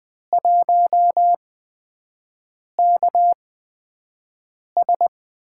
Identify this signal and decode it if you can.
Morse code, heard as a keyed tone, about 20 wpm, spaced as for 5 wpm, 700 Hz. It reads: 1KS
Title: USA Call Signs - 4 Characters - 8x Character Spacing - 20wpm